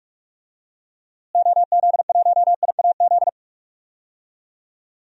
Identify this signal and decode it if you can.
Morse code, heard as a keyed tone, about 43 wpm, 700 Hz. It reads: OZ1IAZ
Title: DXCC-100 Call Signs - 6 Characters - Repeated 2x - 45wpm